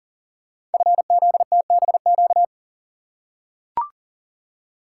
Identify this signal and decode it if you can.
Morse code, heard as a keyed tone, about 41 wpm, 690 Hz. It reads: FZTBQ
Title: Code Groups - 5 Letters - 40wpm